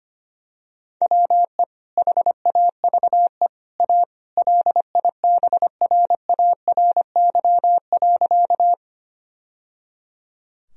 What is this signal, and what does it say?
Morse code, heard as a keyed tone, about 25 wpm, 700 Hz. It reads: WE HAVE A LIBRARY.
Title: Sentences from Top 1000 Words - Review Part 4 - 25wpm